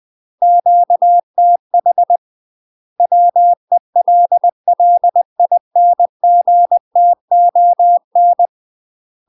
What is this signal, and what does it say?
Morse code, heard as a keyed tone, about 20 wpm, 700 Hz. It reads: QTH WELLINGTON
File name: UK MASTER QSOs TRAINING / 20WPM / UK MASTER QSOs 0001